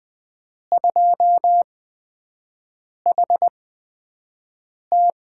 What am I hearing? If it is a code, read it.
Morse code, heard as a keyed tone, about 20 wpm, spaced as for 5 wpm, 700 Hz. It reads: 2HT
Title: USA Call Signs - 8x Character Spacing - 20wpm